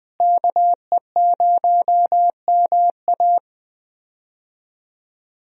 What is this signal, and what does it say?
Morse code, heard as a keyed tone, about 20 wpm, 700 Hz. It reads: KE0MA
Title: USA Call Signs - 5 Characters - Repeated 3x - 20wpm